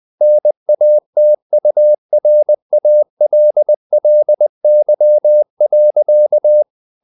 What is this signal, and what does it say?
Morse code, heard as a keyed tone, about 20 wpm, 600 Hz. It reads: NATURALLY.
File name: the old man and the sea / the old man and the sea part 1 20wpm